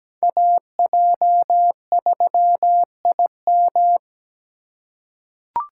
Morse code, heard as a keyed tone, about 17 wpm, 700 Hz. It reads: AJ3IM
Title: USA Call Signs - 5 Characters - 3x Character Spacing - 17wpm